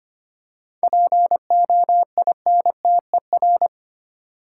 Morse code, heard as a keyed tone, about 25 wpm, 690 Hz. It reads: POINTER
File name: practice / 25wpm / Easy Words with Hints - 25wpm